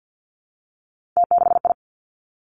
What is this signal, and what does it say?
Morse code, heard as a keyed tone, about 50 wpm, 700 Hz. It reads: T6I
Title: Three Challenging Letter-Number Combinations 50wpm